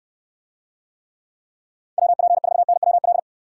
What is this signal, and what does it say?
Morse code, heard as a keyed tone, about 64 wpm, 700 Hz. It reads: WP4NWL